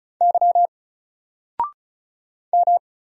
Morse code, heard as a keyed tone, about 34 wpm, 700 Hz. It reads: Y M
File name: Single-Letter-Number-35wpm